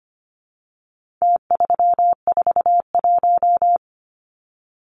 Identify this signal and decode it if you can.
Morse code, heard as a keyed tone, about 25 wpm, 700 Hz. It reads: T341
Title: Numbers - Contest Style - 4-Digit 25wpm